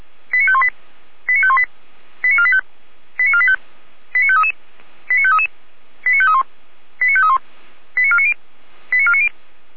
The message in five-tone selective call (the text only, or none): 89518 89518 89485 89485 89533 89533 89531 89531 89490 89490